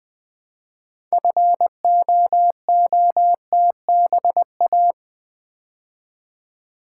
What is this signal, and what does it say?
Morse code, heard as a keyed tone, about 20 wpm, 700 Hz. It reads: FOOTBA
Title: Build A Word - 3 Letters At A Time - 8-Letter Words - 20wpm